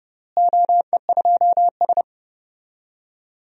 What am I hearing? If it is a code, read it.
Morse code, heard as a keyed tone, about 30 wpm, 700 Hz. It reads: OE2S
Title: DXCC-100 Call Signs - 4 Characters - Repeated 3x - 30wpm